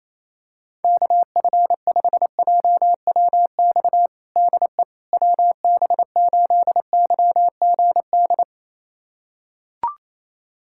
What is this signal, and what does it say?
Morse code, heard as a keyed tone, about 28 wpm, 700 Hz. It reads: KF5JWX DE WB8YGD
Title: USA Callsign DE Callsign 28wpm